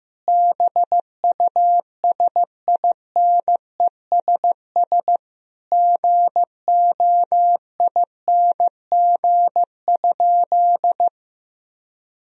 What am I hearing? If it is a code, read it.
Morse code, heard as a keyed tone, about 15 wpm, 700 Hz. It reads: BUSINESS GOING?
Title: Sentences from Top 1000 Words - Review Part 4 - 15wpm